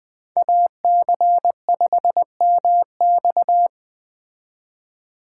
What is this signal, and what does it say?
Morse code, heard as a keyed tone, about 21 wpm, 700 Hz. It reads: AC5MX
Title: Speed Racing - USA 2x2 Call Signs 17wpm